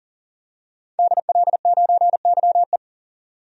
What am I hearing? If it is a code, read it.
Morse code, heard as a keyed tone, about 40 wpm, 700 Hz. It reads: DL9YE
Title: DXCC-100 Call Signs - 5 Characters - 40wpm